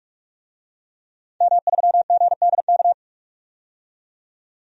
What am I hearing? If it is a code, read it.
Morse code, heard as a keyed tone, about 46 wpm, 700 Hz. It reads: M3GDK
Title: DXCC-100 Call Signs - 5 Characters - 45wpm